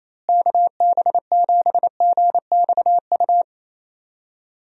Morse code, heard as a keyed tone, about 28 wpm, 700 Hz. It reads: KB7GXU
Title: USA Call Signs - 6 Characters - Repeated 3x - 28wpm